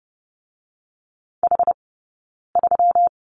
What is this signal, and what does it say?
Morse code, heard as a keyed tone, about 30 wpm, 700 Hz. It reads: H 3
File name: Lesson 20d - Number 3 - Two Characters 30wpm